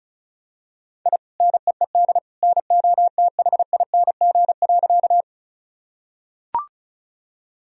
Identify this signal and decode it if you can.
Morse code, heard as a keyed tone, about 36 wpm, 700 Hz. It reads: I NEED NOTHING.
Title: Sentences from Top 1000 Words - Review Part 1 - 35wpm